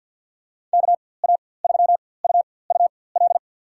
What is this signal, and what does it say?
Morse code, heard as a keyed tone, about 47 wpm, spaced as for 23 wpm, 700 Hz. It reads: KA3UUL
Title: USA Call Signs - 6 Characters - 4x Character Spacing - 50wpm